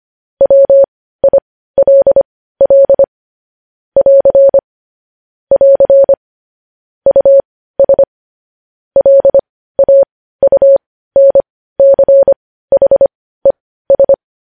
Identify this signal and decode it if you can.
Morse code, heard as a keyed tone, about 25 wpm, spaced as for 15 wpm, 550 Hz. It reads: WILL <AR> <AR> US LAUNCHES
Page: news headlines in morse code at 15 wpm